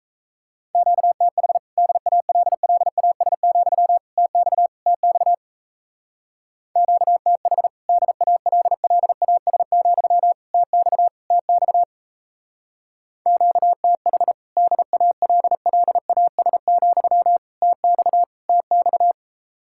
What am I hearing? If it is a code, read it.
Morse code, heard as a keyed tone, about 33 wpm, 700 Hz. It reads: QTH DALLAS, TX TX QTH DALLAS, TX TX QTH DALLAS, TX TX
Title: Speed-Racing Common QSO Phrases Encore - 28wpm